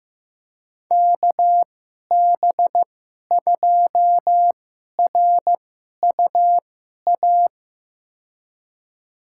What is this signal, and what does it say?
Morse code, heard as a keyed tone, about 15 wpm, spaced as for 11 wpm, 700 Hz. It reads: KB2RUA
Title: USA Call Signs - 2x Character Spacing - 15wpm